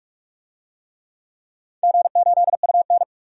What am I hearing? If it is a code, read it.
Morse code, heard as a keyed tone, about 46 wpm, 700 Hz. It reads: G8UN
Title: DXCC-100 Call Signs - 4 Characters - 45wpm